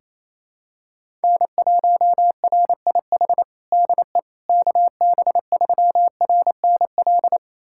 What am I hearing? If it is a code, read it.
Morse code, heard as a keyed tone, about 28 wpm, 700 Hz. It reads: N1RIH DE KB3RNL
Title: USA Callsign DE Callsign 28wpm